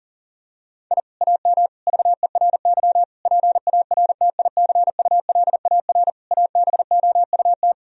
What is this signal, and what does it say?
Morse code, heard as a keyed tone, about 40 wpm, 700 Hz. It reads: I AM VERY PARTICULAR ABOUT
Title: Sentences from Top 1000 Words - Review Part 4 - 40wpm